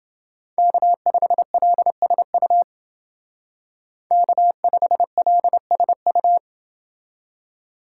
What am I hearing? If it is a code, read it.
Morse code, heard as a keyed tone, about 28 wpm, 700 Hz. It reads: K5LSU K5LSU